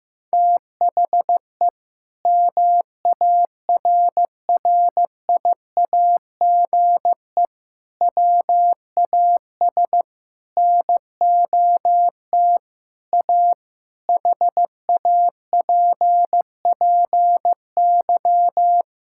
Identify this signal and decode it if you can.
Morse code, heard as a keyed tone, about 15 wpm, 700 Hz. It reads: THE MARRIAGE WAS NOT A HAPPY
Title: Sentences from Top 1000 Words - Review Part 5 - 15wpm